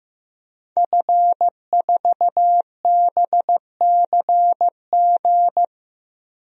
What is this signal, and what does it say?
Morse code, heard as a keyed tone, about 15 wpm, 700 Hz. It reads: F4BCG
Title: DXCC-100 Call Signs - 5 Characters - 15wpm